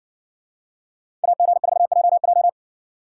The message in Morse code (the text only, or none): AG4JJ